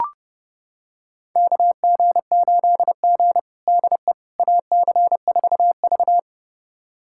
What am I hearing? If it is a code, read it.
Morse code, heard as a keyed tone, about 30 wpm, 700 Hz. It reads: KG8G DE AC4V
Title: USA Callsign DE Callsign - 4 Characters - 30wpm